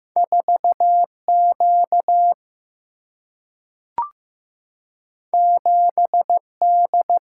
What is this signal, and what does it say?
Morse code, heard as a keyed tone, about 15 wpm, 700 Hz. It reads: 4 Q 7 D
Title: Single-Letter Single-Number 15wpm